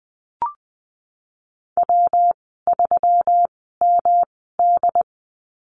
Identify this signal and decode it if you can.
Morse code, heard as a keyed tone, about 19 wpm, spaced as for 14 wpm, 700 Hz. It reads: W3MD
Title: 161 Lesson 26g - Letter g - Call Signs 20wpm